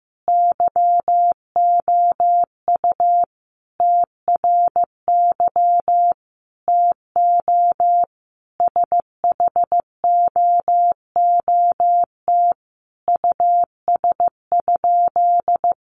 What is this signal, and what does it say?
Morse code, heard as a keyed tone, about 15 wpm, 700 Hz. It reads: YOU TRY TO SHOOT US?
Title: Sentences from Top 1200 Words 15wpm